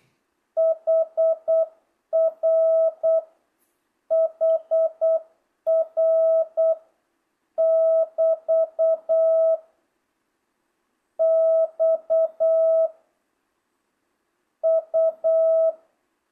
Morse code, heard as a keyed tone, about 8 wpm, 660 Hz. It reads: HR HR <BT> X U